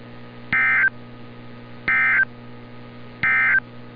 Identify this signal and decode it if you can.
Emergency Alert System (SAME) tones, heard as harsh buzzing bursts. It is heard from 0.4 to 3.7 s.
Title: NOAA Weather Radio - Altoona, PA